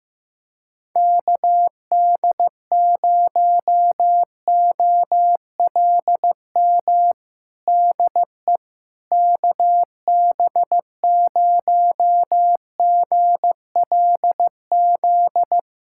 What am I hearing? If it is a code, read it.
Morse code, heard as a keyed tone, about 15 wpm, 700 Hz. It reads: KD0OLM DE KB0GLZ
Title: USA Callsign DE Callsign - 6 Characters - 15wpm